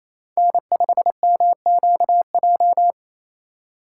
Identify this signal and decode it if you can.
Morse code, heard as a keyed tone, about 28 wpm, 700 Hz. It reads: N5MQJ
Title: USA Call Signs - 5 Characters - 2x Character Spacing - 28wpm